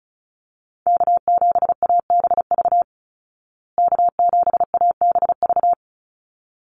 Morse code, heard as a keyed tone, about 34 wpm, 700 Hz. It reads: K7ABV K7ABV
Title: Call Signs Repeated 3x - 35wpm